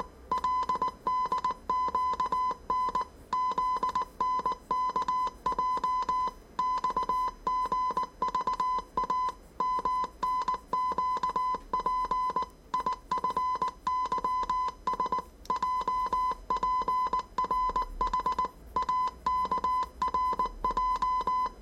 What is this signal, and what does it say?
Morse code, heard as a keyed tone, about 19 wpm, 1010 Hz. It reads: LDQN ZNKJ XGVA MNQP IFYS JPRH AKRJ
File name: 业余无线电 / ●未分类 / 字码80